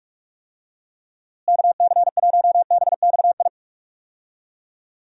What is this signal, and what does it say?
Morse code, heard as a keyed tone, about 42 wpm, 700 Hz. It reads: KC1BXI
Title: USA Call Signs - 6 Characters - Repeated 2x - 45wpm